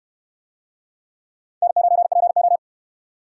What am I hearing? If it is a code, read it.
Morse code, heard as a keyed tone, about 68 wpm, 700 Hz. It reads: N9PO